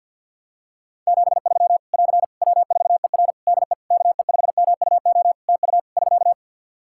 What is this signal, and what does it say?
Morse code, heard as a keyed tone, about 50 wpm, 700 Hz. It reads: 73 <AR> W4ER DE KE5MUO TU <SK>
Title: K1USN SST QSO Practice 50wpm